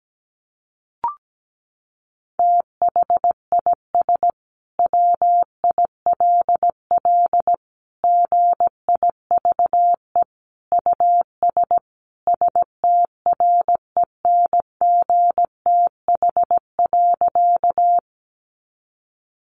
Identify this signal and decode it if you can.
Morse code, heard as a keyed tone, about 17 wpm, 700 Hz. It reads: THIS WILL GIVE US STRENGTH.
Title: Sentences from Top 1300 Words 17wpm